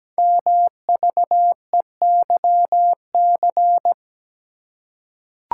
Morse code, heard as a keyed tone, about 17 wpm, 700 Hz. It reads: MVEYC
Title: Code Groups - 5 Letters - 17wpm